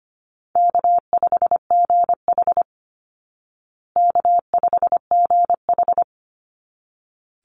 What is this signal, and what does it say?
Morse code, heard as a keyed tone, about 25 wpm, 700 Hz. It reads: K5GH K5GH
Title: Easy Call Signs Repeated 2x - 25wpm